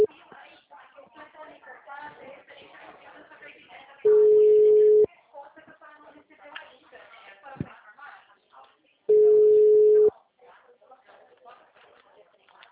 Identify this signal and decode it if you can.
Ringback tone (Europe, 425 Hz), with speech.
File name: out- -2009- - - .49316